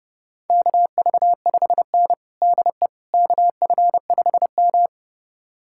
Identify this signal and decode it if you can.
Morse code, heard as a keyed tone, about 30 wpm, 700 Hz. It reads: KV5N DE KF5M
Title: USA Callsign DE Callsign - 4 Characters - 30wpm